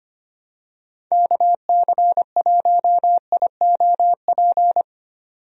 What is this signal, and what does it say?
Morse code, heard as a keyed tone, about 25 wpm, 700 Hz. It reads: KC1IOP
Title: USA Call Signs - 6 Characters - 25wpm